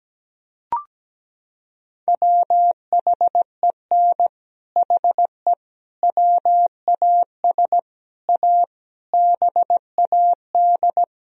Morse code, heard as a keyed tone, about 17 wpm, 700 Hz. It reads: WHEN HE WAS A BAD
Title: Sentences from Top 1000 Words - Review Part 3 - 17wpm